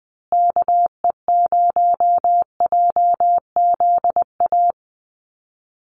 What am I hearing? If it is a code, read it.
Morse code, heard as a keyed tone, about 20 wpm, 700 Hz. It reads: KE0JZA
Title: USA Call Signs Repeated 2x 20wpm